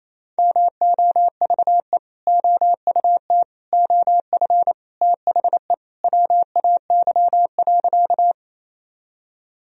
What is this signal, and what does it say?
Morse code, heard as a keyed tone, about 28 wpm, 700 Hz. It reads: MOVE OUT OF THE WAY.
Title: Sentences from Top 1000 Words - Review Part 1 - 28wpm